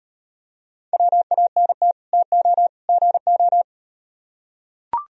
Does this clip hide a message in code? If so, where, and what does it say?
Yes: Morse code from 0.9 to 3.6 s, heard as a keyed tone, about 38 wpm, 700 Hz. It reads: WANT TO GO